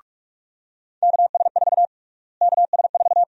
Morse code, heard as a keyed tone, about 42 wpm, 700 Hz. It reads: KS4 KS4
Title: USA Call Sign Prefixes - Repeated 2x - 45wpm